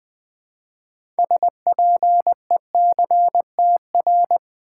Morse code, heard as a keyed tone, about 20 wpm, 700 Hz. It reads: SPECTR
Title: Build A Word - 3 Letters At A Time - 8-Letter Words - 20wpm